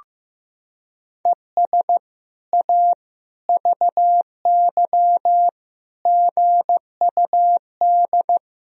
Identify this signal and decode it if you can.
Morse code, heard as a keyed tone, about 15 wpm, 700 Hz. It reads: ES A VY GUD
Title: Sentences from Top 200 Words and Abbreviations 15wpm